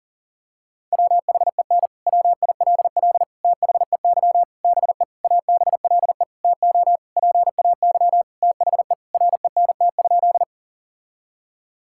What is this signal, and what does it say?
Morse code, heard as a keyed tone, about 40 wpm, 700 Hz. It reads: WHEN WILL THEY BE ABLE TO PAY THE RENT?
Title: Sentences from Top 1000 Words - Review Part 5 - 40wpm